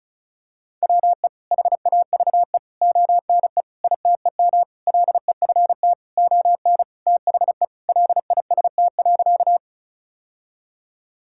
Morse code, heard as a keyed tone, about 36 wpm, 700 Hz. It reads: WE HAVE ONE ITEM LEFT ON THE LIST.
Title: Sentences from Top 1000 Words - Review Part 6 - 35wpm